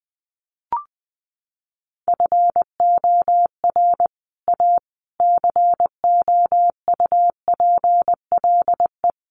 Morse code, heard as a keyed tone, about 20 wpm, 700 Hz. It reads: FOR A COUPLE
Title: Sets of 5 Words Encore2 20wpm